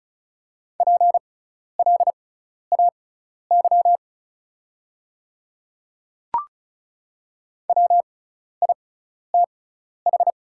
Morse code, heard as a keyed tone, about 34 wpm, spaced as for 12 wpm, 690 Hz. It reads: PLAY WITH
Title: 6x Character Spacing - Top 300 Words - 35wpm